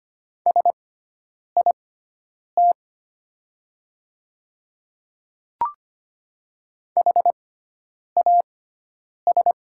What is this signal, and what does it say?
Morse code, heard as a keyed tone, about 25 wpm, spaced as for 9 wpm, 700 Hz. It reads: SIT HAS
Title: 6x Character Spacing - Top 500 Words - 25wpm